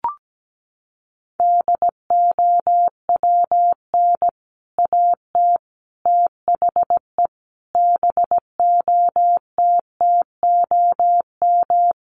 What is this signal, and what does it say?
Morse code, heard as a keyed tone, about 17 wpm, 700 Hz. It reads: DOWN AT THE BOTTOM
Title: Sentences from Top 1100 Words 17wpm